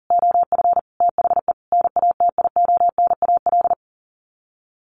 Morse code, heard as a keyed tone, about 40 wpm, 700 Hz. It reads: OF THE NATIONAL
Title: Sets of 5 Words Encore2 40wpm